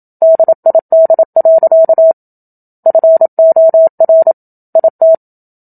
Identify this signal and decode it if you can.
Morse code, heard as a keyed tone, about 27 wpm, 650 Hz. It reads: DID. FOR IT
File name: Book Two / 01-Chapter 1 - 27 WPM